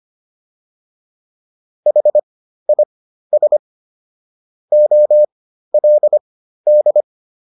Morse code, heard as a keyed tone, about 25 wpm, spaced as for 13 wpm, 600 Hz. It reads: HIS OLD
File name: PR202 13